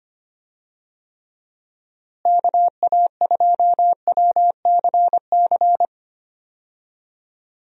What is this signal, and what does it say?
Morse code, heard as a keyed tone, about 25 wpm, 700 Hz. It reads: KA2WCC
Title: USA Call Signs - 6 Characters - Repeated 3x - 25wpm